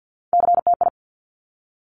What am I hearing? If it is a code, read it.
Morse code, heard as a keyed tone, about 50 wpm, 700 Hz. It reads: CTI
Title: Top Three Letter Combinations in Words 50wpm